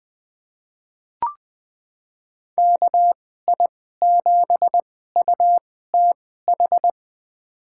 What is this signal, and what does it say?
Morse code, heard as a keyed tone, about 20 wpm, spaced as for 14 wpm, 700 Hz. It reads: KI7UTH
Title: USA Call Signs - 6 Characters - 2x Character Spacing - 20wpm